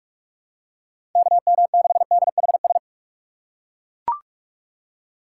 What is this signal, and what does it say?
Morse code, heard as a keyed tone, about 40 wpm, 700 Hz. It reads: KM6DHS